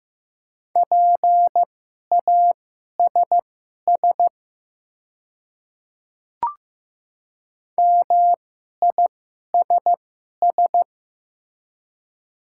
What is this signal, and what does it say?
Morse code, heard as a keyed tone, about 15 wpm, spaced as for 11 wpm, 700 Hz. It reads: PASS MISS
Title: 2x Character Spacing - Top 500 Words - 15wpm